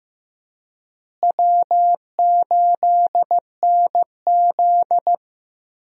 Morse code, heard as a keyed tone, about 15 wpm, 700 Hz. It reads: W8NZ
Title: USA Call Signs - 4 Characters - Repeated 3x - 15wpm